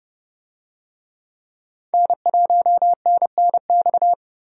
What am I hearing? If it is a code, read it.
Morse code, heard as a keyed tone, about 30 wpm, 700 Hz. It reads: N1NNX
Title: USA Call Signs - 5 Characters - Repeated 3x - 30wpm